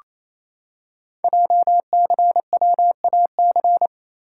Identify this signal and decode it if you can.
Morse code, heard as a keyed tone, about 27 wpm, 700 Hz. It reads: JCWAC
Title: Code Groups - 5 Letters - 28wpm